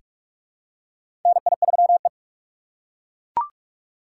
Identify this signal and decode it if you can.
Morse code, heard as a keyed tone, about 42 wpm, 700 Hz. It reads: NI3E
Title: USA Call Signs - 4 Characters - 4x Character Spacing - 45wpm